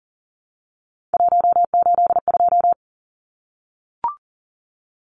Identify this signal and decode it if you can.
Morse code, heard as a keyed tone, about 39 wpm, 700 Hz. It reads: 182
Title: Numbers - 3-Digit 40wpm